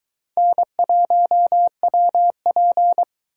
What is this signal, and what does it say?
Morse code, heard as a keyed tone, about 23 wpm, 700 Hz. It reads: N1WP